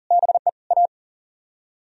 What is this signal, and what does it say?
Morse code, heard as a keyed tone, about 51 wpm, 700 Hz. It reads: BE A